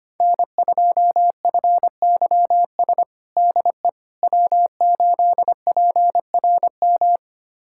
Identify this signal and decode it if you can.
Morse code, heard as a keyed tone, about 25 wpm, 700 Hz. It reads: N2FYS DE W8PRM
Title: USA Callsign DE Callsign - 5 Characters - 25wpm